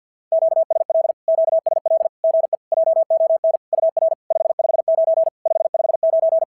Morse code, heard as a keyed tone, about 49 wpm, 690 Hz. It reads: QSL QSL GE JON UR 559 559
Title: SOTA QSO Practice 50wpm